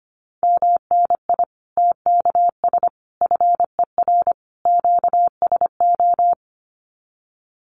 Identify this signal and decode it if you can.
Morse code, heard as a keyed tone, about 25 wpm, 700 Hz. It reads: MNI TKS FER QSO
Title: Common QSO Phrases 25wpm